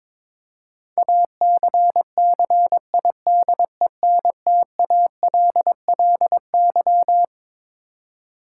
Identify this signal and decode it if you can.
Morse code, heard as a keyed tone, about 21 wpm, 700 Hz. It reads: ACCIDENTALLY
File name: Twelve Letter Words - 22wpm